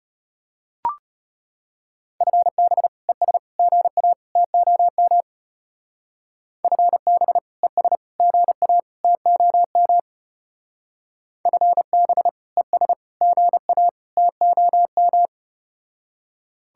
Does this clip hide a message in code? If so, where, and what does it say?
Yes: Morse code from 2.2 to 15.3 s, heard as a keyed tone, about 33 wpm, 700 Hz. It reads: FB ES GA TOM FB ES GA TOM FB ES GA TOM